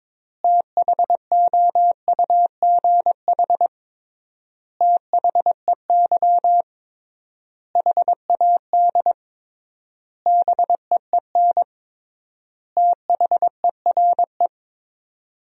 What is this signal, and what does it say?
Morse code, heard as a keyed tone, about 22 wpm, 700 Hz. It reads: THOUGH THEY HAD BEEN THERE
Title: 3x Word-Spacing - Sentences from Top 400 Words - 22wpm